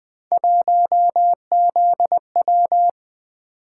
Morse code, heard as a keyed tone, about 20 wpm, 700 Hz. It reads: 1ZW
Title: Three Letter-Number Combinations - 20wpm